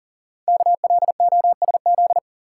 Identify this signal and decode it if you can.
Morse code, heard as a keyed tone, about 40 wpm, 690 Hz. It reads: KLOSZ
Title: Code Groups - 5 Letters - 40wpm